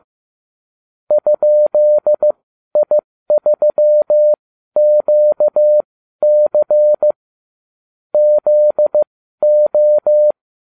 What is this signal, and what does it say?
Morse code, heard as a keyed tone, about 15 wpm, spaced as for 12 wpm, 600 Hz. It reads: ?I3QC ZO